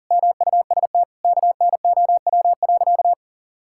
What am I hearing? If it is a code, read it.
Morse code, heard as a keyed tone, about 40 wpm, 700 Hz. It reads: MUST KNOW.
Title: Sentences from Top 1000 Words - Review Part 6 - 40wpm